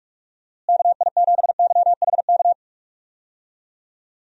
Morse code, heard as a keyed tone, about 46 wpm, 700 Hz. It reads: KI7YHK
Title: USA Call Signs - 6 Characters - Repeated 3x - 45wpm